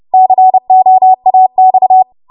Morse code, heard as a keyed tone, about 30 wpm, 750 Hz. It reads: COAX
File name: Morse / 30WPM